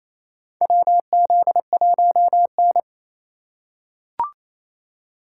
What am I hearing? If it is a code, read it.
Morse code, heard as a keyed tone, about 28 wpm, 700 Hz. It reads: WZ1N
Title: USA Call Signs - 2x Character Spacing - 28wpm